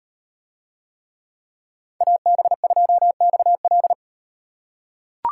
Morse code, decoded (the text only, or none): AB2XL